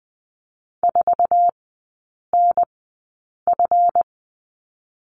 Morse code, heard as a keyed tone, about 20 wpm, spaced as for 8 wpm, 700 Hz. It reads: 4NF
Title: 166 Lesson 27e - Number 7 - Three Characters 20wpm